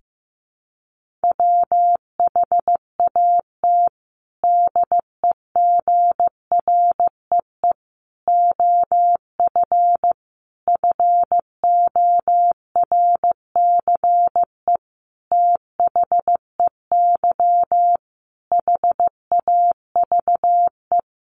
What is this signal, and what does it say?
Morse code, heard as a keyed tone, about 14 wpm, 700 Hz. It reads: WHAT DEGREE OF FORCE THEY HAVE
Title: Sentences from Top 1500 Words 15wpm